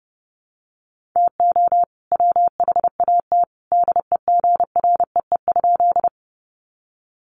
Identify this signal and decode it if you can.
Morse code, heard as a keyed tone, about 30 wpm, 700 Hz. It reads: TO WHAT DEGREE?
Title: Sentences from Top 1500 Words 30wpm